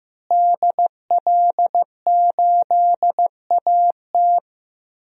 Morse code, heard as a keyed tone, about 15 wpm, 700 Hz. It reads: DL8AT
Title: DXCC-100 Call Signs - 5 Characters - Repeated 3x - 15wpm